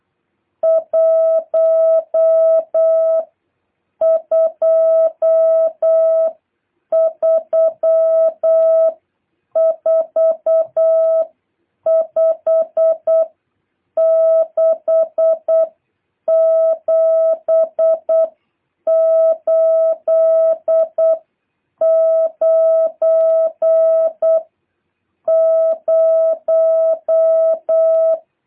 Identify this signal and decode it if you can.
Morse code, heard as a keyed tone, about 8 wpm, spaced as for 7 wpm, 650 Hz. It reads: 1234567890
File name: jh8cbh / 2023cw